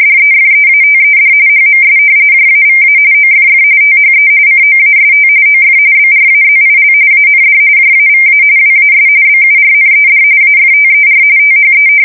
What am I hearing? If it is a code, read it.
RTTY, about 45.45 baud. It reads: DEFEATED HMS JAVA OFF THE COAST OF BRAZIL AND JAVA WAS BURNED AFTER THE
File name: fsk170 / usnavy2